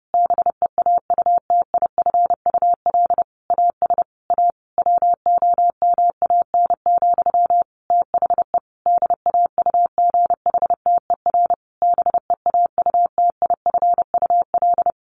Morse code, heard as a keyed tone, about 30 wpm, 700 Hz. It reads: BEAUTIFUL AS A WOMAN, THE DAUGHTER BEAUTIFUL
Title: Sentences from Top 1500 Words 30wpm